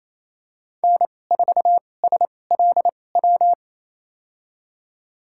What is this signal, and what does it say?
Morse code, heard as a keyed tone, about 28 wpm, spaced as for 20 wpm, 700 Hz. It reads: N4SLW
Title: USA Call Signs - 2x Character Spacing - 28wpm